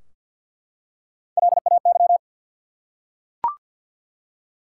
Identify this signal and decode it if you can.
Morse code, heard as a keyed tone, about 52 wpm, 700 Hz. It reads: LAY
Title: Three Letter Words - 50wpm